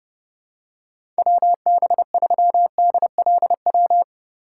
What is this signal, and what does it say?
Morse code, heard as a keyed tone, about 30 wpm, 700 Hz. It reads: WB3DLW
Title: USA Call Signs - 6 Characters - 2x Character Spacing - 30wpm